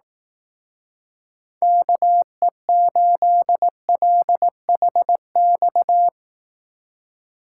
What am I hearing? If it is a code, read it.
Morse code, heard as a keyed tone, about 18 wpm, 700 Hz. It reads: KE8LHX
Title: Speed Racing - USA Call Signs 15wpm